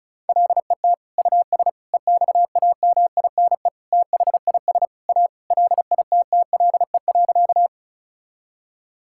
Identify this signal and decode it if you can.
Morse code, heard as a keyed tone, about 34 wpm, 700 Hz. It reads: LET US EXAMINE THIS A LITTLE.
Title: Sentences from Top 1600 Words 35wpm